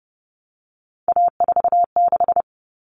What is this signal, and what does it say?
Morse code, heard as a keyed tone, about 30 wpm, 700 Hz. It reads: A46
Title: Common Cut Numbers - 3-Digit 30wpm